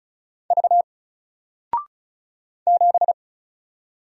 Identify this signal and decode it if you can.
Morse code, heard as a keyed tone, about 36 wpm, 700 Hz. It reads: V 7
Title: Single Letter-Number Alt 35wpm